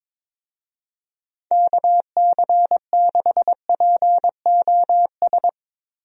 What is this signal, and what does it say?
Morse code, heard as a keyed tone, about 22 wpm, 700 Hz. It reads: KC6POS
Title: USA Call Signs - 6 Characters - Repeated 2x - 22wpm